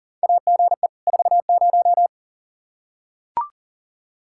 Morse code, heard as a keyed tone, about 40 wpm, 700 Hz. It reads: AGE 40
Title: Practice QSO - Exchange 3 - More Info - 40wpm